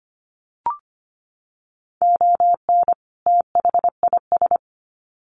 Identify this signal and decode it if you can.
Morse code, heard as a keyed tone, about 25 wpm, 700 Hz. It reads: ON THIS
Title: Sets of 2 Words 25wpm